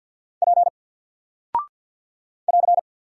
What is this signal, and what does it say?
Morse code, heard as a keyed tone, about 50 wpm, 700 Hz. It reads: P <AR>
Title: Single Character (Letter:Number:Punctuation:Prosign) 50wpm